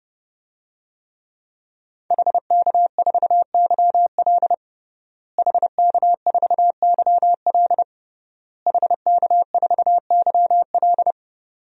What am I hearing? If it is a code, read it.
Morse code, heard as a keyed tone, about 29 wpm, 700 Hz. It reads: HK4YL HK4YL HK4YL
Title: DXCC-100 Call Signs - 5 Characters - Repeated 3x - 30wpm